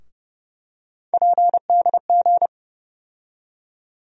Morse code, heard as a keyed tone, about 30 wpm, 700 Hz. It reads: PDG
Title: USA Call Sign Postfixes - 3 Letters 30wpm